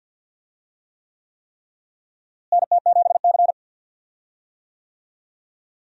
Morse code, heard as a keyed tone, about 46 wpm, 700 Hz. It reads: NT7C
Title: USA Call Signs - 4 Characters - Repeated 3x - 50wpm